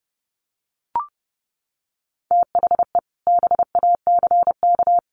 Morse code, heard as a keyed tone, about 28 wpm, 700 Hz. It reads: THE BACK
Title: Sets of 4 Words 30wpm